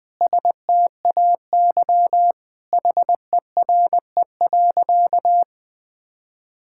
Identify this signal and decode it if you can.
Morse code, heard as a keyed tone, about 19 wpm, 700 Hz. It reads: STAY HERE.
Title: Sentences with 4-Word Phrases - Spoken Beginning - 20wpm